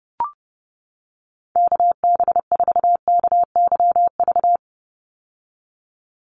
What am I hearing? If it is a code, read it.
Morse code, heard as a keyed tone, about 30 wpm, 700 Hz. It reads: KB4KYV
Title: USA Call Signs 30wpm